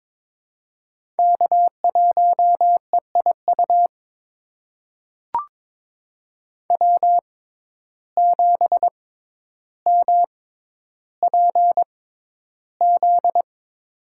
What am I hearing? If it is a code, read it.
Morse code, heard as a keyed tone, about 21 wpm, spaced as for 11 wpm, 700 Hz. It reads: K1EIU W7MPZ
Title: USA Call Signs - 5 Characters - 6x Character Spacing - 22wpm